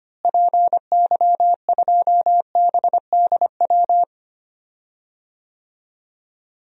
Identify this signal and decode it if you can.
Morse code, heard as a keyed tone, about 25 wpm, 700 Hz. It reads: PY2BDW
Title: DXCC-100 Call Signs - 6 Characters - Repeated 3x - 25wpm